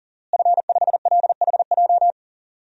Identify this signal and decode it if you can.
Morse code, heard as a keyed tone, about 37 wpm, 700 Hz. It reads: F5LHJ
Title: DXCC-100 Call Signs - 5 Characters - Repeated 3x - 40wpm